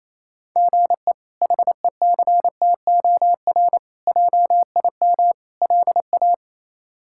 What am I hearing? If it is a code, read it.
Morse code, heard as a keyed tone, about 28 wpm, 700 Hz. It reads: GE HECTOR JIM LA
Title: K1USN SST QSO Practice - 28wpm